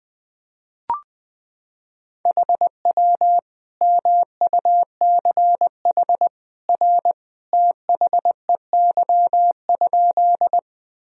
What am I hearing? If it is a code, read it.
Morse code, heard as a keyed tone, about 20 wpm, 700 Hz. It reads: HW MUCH R THEY?
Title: Sentences from Top 200 Words and Abbreviations 20wpm